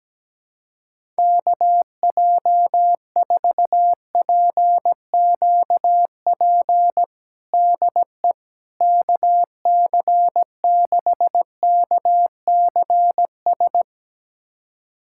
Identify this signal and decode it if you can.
Morse code, heard as a keyed tone, about 17 wpm, 700 Hz. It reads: KJ4PQP DE KC6KCS
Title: USA Callsign DE Callsign - 6 Characters - 17wpm